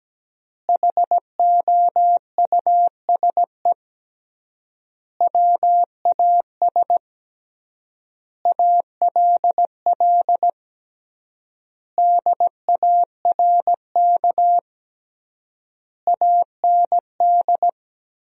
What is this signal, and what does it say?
Morse code, heard as a keyed tone, about 17 wpm, 700 Hz. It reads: HOUSE WAS ALL DARK AND
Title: 3x Word-Spacing - Sentences from Top 500 Words - 17wpm